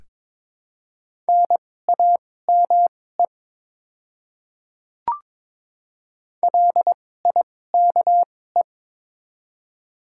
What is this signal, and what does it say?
Morse code, heard as a keyed tone, about 22 wpm, spaced as for 16 wpm, 690 Hz. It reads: NAME LIKE
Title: 2x Character Spacing - Top 300 Words - 22wpm